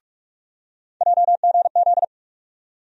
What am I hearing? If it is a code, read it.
Morse code, heard as a keyed tone, about 45 wpm, 700 Hz. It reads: JGZ
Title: USA Call Sign Postfixes - 3 Letters 45wpm